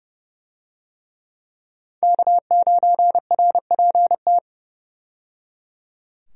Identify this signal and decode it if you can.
Morse code, heard as a keyed tone, about 30 wpm, 700 Hz. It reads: K9RPT
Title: USA Call Signs - 5 Characters - 30wpm